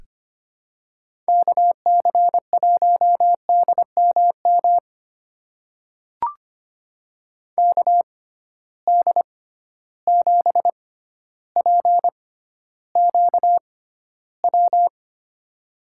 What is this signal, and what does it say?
Morse code, heard as a keyed tone, about 25 wpm, spaced as for 13 wpm, 700 Hz. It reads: KC1DMM KD7PQW
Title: USA Call Signs - 6 Characters - 6x Character Spacing - 25wpm